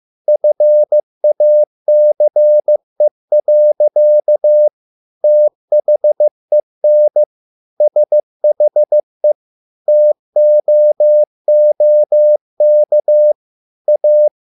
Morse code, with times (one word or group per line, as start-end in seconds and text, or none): 0.3-4.7 FACE.
5.2-7.2 THEN
7.8-9.3 SHE
9.9-13.3 TOOK
13.9-14.3 A